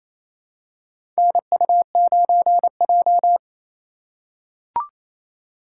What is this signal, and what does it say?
Morse code, heard as a keyed tone, about 28 wpm, 700 Hz. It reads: NU9J
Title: USA Call Signs - 2x Character Spacing - 28wpm